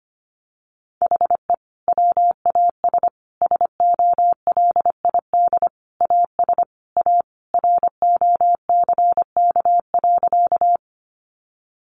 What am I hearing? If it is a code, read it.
Morse code, heard as a keyed tone, about 25 wpm, 700 Hz. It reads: HE WAS SOLID AS A ROCK.
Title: Sentences from Top 1500 Words 25wpm